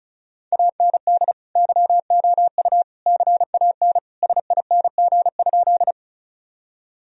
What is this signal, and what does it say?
Morse code, heard as a keyed tone, about 36 wpm, 700 Hz. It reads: AND YOU CAN SING?
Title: Sentences from Top 1000 Words - Review Part 2 - 35wpm